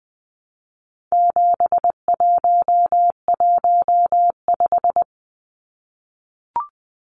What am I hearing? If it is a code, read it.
Morse code, heard as a keyed tone, about 20 wpm, 700 Hz. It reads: 7115
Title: Numbers - 4-Digit 20wpm